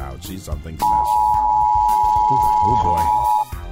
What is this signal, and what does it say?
Emergency Alert System (SAME) tones (the 853 and 960 Hz attention tone), with speech. It is heard from 0.8 to 3.5 s.